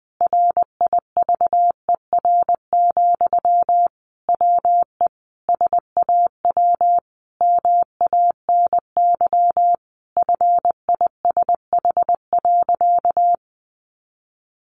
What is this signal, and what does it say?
Morse code, heard as a keyed tone, about 20 wpm, 700 Hz. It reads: RIVER, WE SAW MANY FISH.
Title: Sentences from Top 1500 Words 20wpm